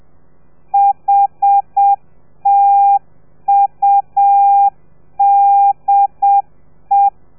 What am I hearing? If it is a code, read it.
Morse code, heard as a keyed tone, about 7 wpm, 800 Hz. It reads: HTUDE